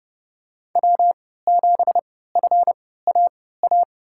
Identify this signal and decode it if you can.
Morse code, heard as a keyed tone, about 30 wpm, spaced as for 17 wpm, 700 Hz. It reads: W7FAA
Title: USA Call Signs - 5 Characters - 3x Character Spacing - 30wpm